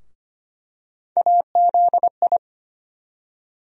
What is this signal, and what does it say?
Morse code, heard as a keyed tone, about 25 wpm, 700 Hz. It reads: AZI
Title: USA Call Sign Postfixes - 3 Letters 25wpm